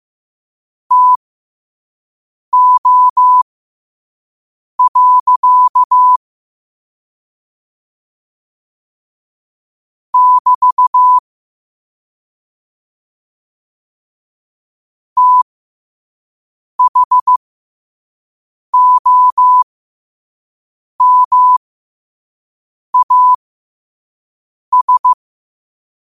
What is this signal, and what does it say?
Morse code, heard as a keyed tone, about 15 wpm, spaced as for 5 wpm, 1000 Hz. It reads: TO. <BT> THOMAS
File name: qotdfiles / 5wpm